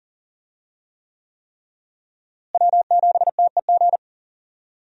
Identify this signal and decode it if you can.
Morse code, heard as a keyed tone, about 40 wpm, 700 Hz. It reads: W7TEG
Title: USA Call Signs - 5 Characters - Repeated 3x - 40wpm